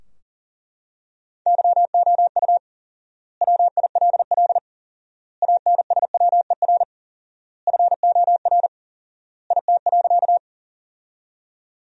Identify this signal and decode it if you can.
Morse code, heard as a keyed tone, about 40 wpm, 700 Hz. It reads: YOU WILL ANSWER FOR IT.
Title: 4x Word-Spacing - Sentences from Top 300 Words - 40wpm